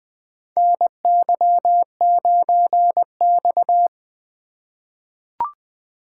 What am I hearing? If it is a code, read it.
Morse code, heard as a keyed tone, about 20 wpm, 700 Hz. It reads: NY9X
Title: K1USN SST QSO Practice - 20wpm